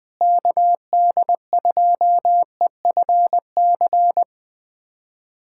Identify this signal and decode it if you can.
Morse code, heard as a keyed tone, about 20 wpm, 700 Hz. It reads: KD2EFC
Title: USA Callsign DE Callsign 20wpm